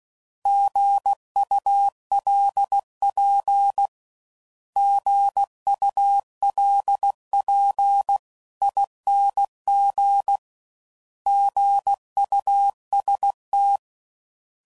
Morse code, heard as a keyed tone, about 16 wpm, 800 Hz. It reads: GULP GULP ING GUST
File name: Prefixes / By First Letter / Gs